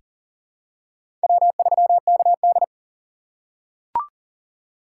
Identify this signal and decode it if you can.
Morse code, heard as a keyed tone, about 40 wpm, 700 Hz. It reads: W3KD
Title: USA Call Signs - 4 Characters - 4x Character Spacing - 40wpm